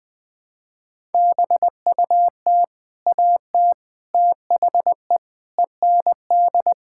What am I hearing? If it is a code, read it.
Morse code, heard as a keyed tone, about 21 wpm, 700 Hz. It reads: BUT AT THE END
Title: Sets of 8 Words - 20wpm